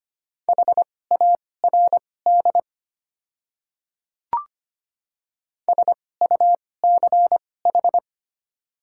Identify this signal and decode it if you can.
Morse code, heard as a keyed tone, about 25 wpm, spaced as for 18 wpm, 690 Hz. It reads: HARD SUCH
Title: 2x Character Spacing - Top 300 Words - 25wpm